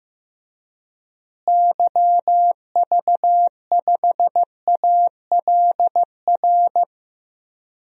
Morse code, heard as a keyed tone, about 16 wpm, 700 Hz. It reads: YV5ALR
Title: DXCC-100 Call Signs - 6 Characters - Repeated 2x - 15wpm